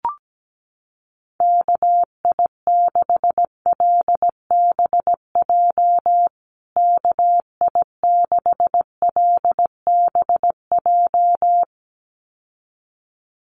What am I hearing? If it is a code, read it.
Morse code, heard as a keyed tone, about 17 wpm, 700 Hz. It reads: KI6LBJ KI6LBJ
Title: USA Call Signs Repeated 2x 17wpm